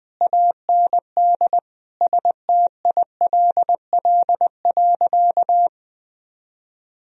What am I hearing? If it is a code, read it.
Morse code, heard as a keyed tone, about 20 wpm, 700 Hz. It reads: AND STILL.
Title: Sentences from Top 1000 Words - Review Part 2 - 20wpm